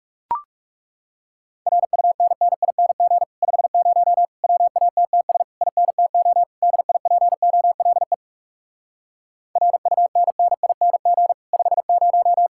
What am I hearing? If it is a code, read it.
Morse code, heard as a keyed tone, about 39 wpm, 700 Hz. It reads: RUNNING 50 WATTS INTO DIPOLE RUNNING 50